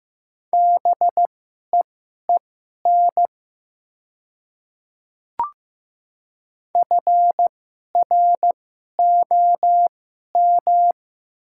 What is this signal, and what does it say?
Morse code, heard as a keyed tone, about 15 wpm, spaced as for 11 wpm, 700 Hz. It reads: BEEN FROM
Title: 2x Character Spacing - Top 200 Words - 15wpm